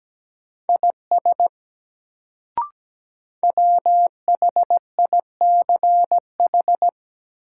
Morse code, heard as a keyed tone, about 17 wpm, 700 Hz. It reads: IS WHICH
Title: Sentences from Top 100 Words - New Word Drill - 17wpm